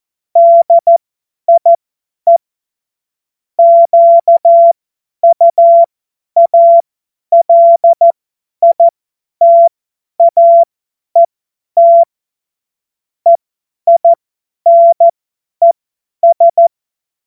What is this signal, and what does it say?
Morse code, heard as a keyed tone, about 14 wpm, spaced as for 10 wpm, 680 Hz. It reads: DIE QUALITAET EINES
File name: Witz-16-September-2025-50-F70